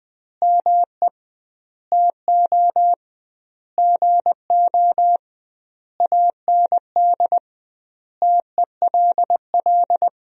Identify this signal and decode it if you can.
Morse code, heard as a keyed tone, about 20 wpm, 700 Hz. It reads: ME TO GO AND TELL
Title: 2x Word-Spacing - Sentences from Top 200 Words - 20wpm